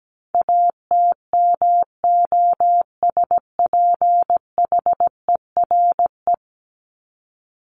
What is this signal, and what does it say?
Morse code, heard as a keyed tone, about 17 wpm, 700 Hz. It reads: ATMOSPHERE
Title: Top 1501-1750 Words 17wpm